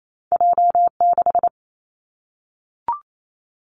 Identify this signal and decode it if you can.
Morse code, heard as a keyed tone, about 28 wpm, 700 Hz. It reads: J 6
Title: Two Challenging Letter-Number Combinations 28wpm